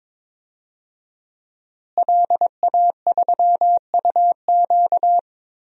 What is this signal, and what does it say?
Morse code, heard as a keyed tone, about 22 wpm, 700 Hz. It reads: LA3UQ
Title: DXCC-100 Call Signs - 5 Characters - Repeated 2x - 22wpm